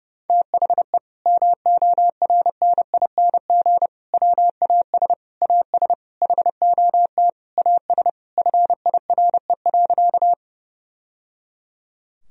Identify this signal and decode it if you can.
Morse code, heard as a keyed tone, about 30 wpm, 700 Hz. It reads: THE MORNING WAS AS HOT AS FIRE.
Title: Sentences from Top 1000 Words - Review Part 3 - 30wpm